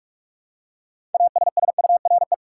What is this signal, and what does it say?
Morse code, heard as a keyed tone, about 46 wpm, 700 Hz. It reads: ASSURE
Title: Six Letter Words - 45wpm